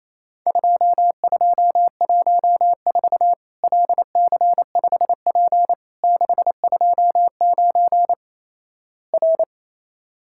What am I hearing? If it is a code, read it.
Morse code, heard as a keyed tone, about 28 wpm, 700 Hz. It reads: 2214 LC5P 629 R
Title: Worked All Europe DX Contest - QTC Exchange - 28wpm